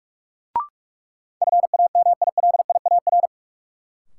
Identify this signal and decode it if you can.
Morse code, heard as a keyed tone, about 42 wpm, 700 Hz. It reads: FAMILIAR
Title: Sentences from Top 1000 Words - New Word Drill - 45wpm